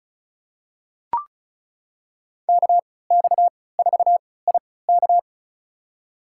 Morse code, heard as a keyed tone, about 34 wpm, spaced as for 20 wpm, 700 Hz. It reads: KX4IK
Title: USA Call Signs - 5 Characters - 3x Character Spacing - 35wpm